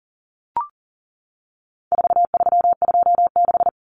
Morse code, heard as a keyed tone, about 40 wpm, 700 Hz. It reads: 4326
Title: Common Cut Numbers - 4-Digit 40wpm